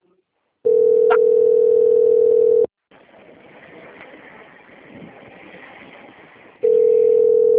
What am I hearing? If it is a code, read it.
Ringback tone (North America), 440 + 480 Hz.
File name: out- - - - - .12344